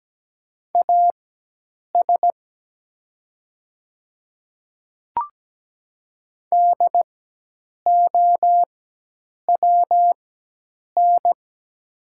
Morse code, heard as a keyed tone, about 17 wpm, spaced as for 8 wpm, 700 Hz. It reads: AS DOWN
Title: 4x Character Spacing - Top 100 Words - 17wpm